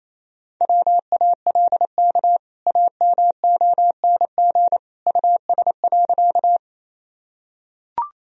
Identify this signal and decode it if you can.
Morse code, heard as a keyed tone, about 27 wpm, 700 Hz. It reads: WALK AMONG US.
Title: Sentences from Top 1000 Words - Review Part 3 - 28wpm